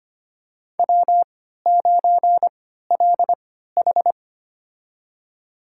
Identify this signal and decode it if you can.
Morse code, heard as a keyed tone, about 25 wpm, spaced as for 14 wpm, 700 Hz. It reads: W9LH
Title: USA Call Signs - 4 Characters - 3x Character Spacing - 25wpm